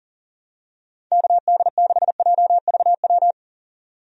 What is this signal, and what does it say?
Morse code, heard as a keyed tone, about 40 wpm, 700 Hz. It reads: KD6JVW
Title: USA Call Signs - 6 Characters - 40wpm